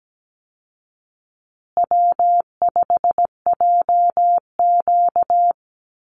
Morse code, heard as a keyed tone, about 17 wpm, 700 Hz. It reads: W5JQ
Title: Call Signs Repeated 2x - 17wpm